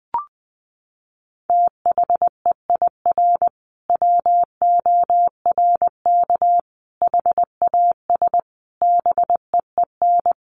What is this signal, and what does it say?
Morse code, heard as a keyed tone, about 20 wpm, 700 Hz. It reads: THEIR WORK HAS BEEN
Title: Sentences from Top 1500 Words 20wpm